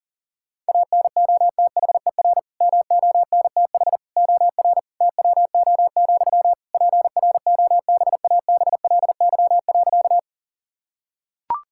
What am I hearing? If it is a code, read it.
Morse code, heard as a keyed tone, about 40 wpm, 700 Hz. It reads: ANOTHER MONTH OR TWO, PROBABLY.
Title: Sentences from Top 1000 Words - Review Part 5 - 40wpm